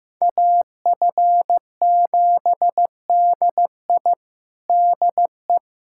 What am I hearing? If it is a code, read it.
Morse code, heard as a keyed tone, about 15 wpm, 700 Hz. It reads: AF7DI DE
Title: USA Callsign DE Callsign - 5 Characters - 15wpm